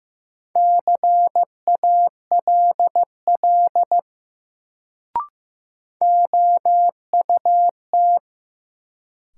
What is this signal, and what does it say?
Morse code, heard as a keyed tone, about 15 wpm, 700 Hz. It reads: CALL OUT
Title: Sentences from Top 100 Words - New Word Drill - 15wpm